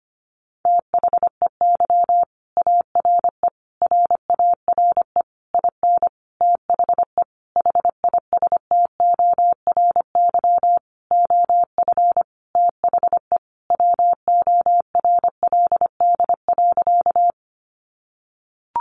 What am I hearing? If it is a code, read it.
Morse code, heard as a keyed tone, about 25 wpm, 700 Hz. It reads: THEY ARE RARE IN THE HISTORY OF THE WORLD.
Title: Sentences from Top 1100 Words 25wpm